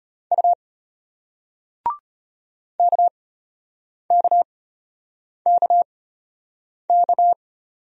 Morse code, heard as a keyed tone, about 34 wpm, spaced as for 8 wpm, 700 Hz. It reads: U KKKK